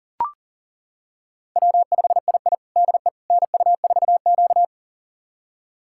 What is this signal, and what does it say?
Morse code, heard as a keyed tone, about 40 wpm, 700 Hz. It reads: W5II DE NU4Q
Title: USA Callsign DE Callsign - 4 Characters - 40wpm